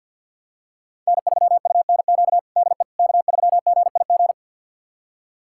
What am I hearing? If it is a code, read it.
Morse code, heard as a keyed tone, about 52 wpm, 700 Hz. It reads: N3UNQ DE K3GIG
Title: USA Callsign DE Callsign - 5 Characters - 50wpm